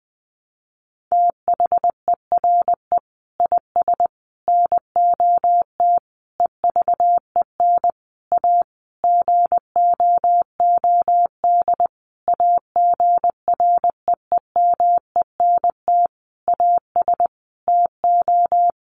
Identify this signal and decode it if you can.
Morse code, heard as a keyed tone, about 20 wpm, 700 Hz. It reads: THERE IS NOT EVEN A GOOD AGREEMENT AS TO
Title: Sentences from Top 1500 Words 20wpm